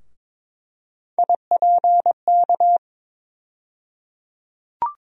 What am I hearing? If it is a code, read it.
Morse code, heard as a keyed tone, about 22 wpm, 700 Hz. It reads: IPK
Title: USA Call Sign Postfixes - 3 Letters 22wpm